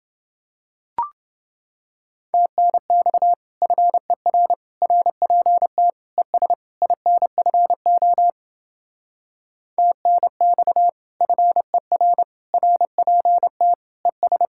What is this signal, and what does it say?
Morse code, heard as a keyed tone, about 29 wpm, 700 Hz. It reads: TNX FER RPT ES INFO TNX FER RPT ES